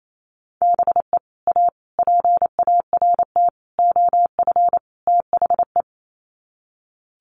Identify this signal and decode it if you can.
Morse code, heard as a keyed tone, about 28 wpm, 690 Hz. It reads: BE A PART OF THE
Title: Sets of 5 Words Encore 28wpm